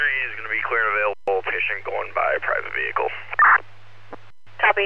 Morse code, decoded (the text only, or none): none